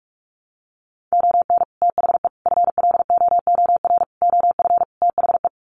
Morse code, heard as a keyed tone, about 44 wpm, 700 Hz. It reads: ON THE FLOOR OF THE
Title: Sets of 5 Words 45wpm